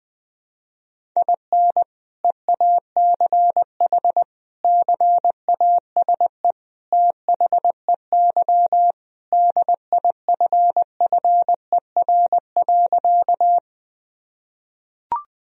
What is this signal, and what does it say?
Morse code, heard as a keyed tone, about 20 wpm, 700 Hz. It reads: IN EACH CASE THEY DIFFER.
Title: Sentences from Top 1000 Words - Review Part 1 - 20wpm